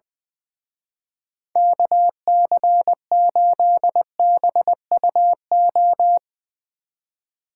Morse code, heard as a keyed tone, about 22 wpm, 700 Hz. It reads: KC8BUO
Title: Speed Racing - USA Call Signs 20wpm